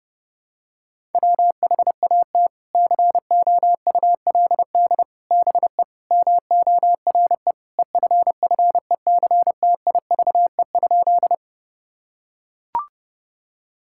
Morse code, heard as a keyed tone, about 32 wpm, 700 Hz. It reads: WHAT COULD BE MORE EFFECTIVE?
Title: Sentences from Top 1000 Words - Review Part 6 - 30wpm